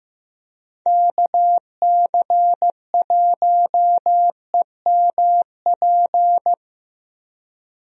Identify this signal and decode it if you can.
Morse code, heard as a keyed tone, about 15 wpm, 700 Hz. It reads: KC1EMP
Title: USA Call Signs - 6 Characters - Repeated 2x - 15wpm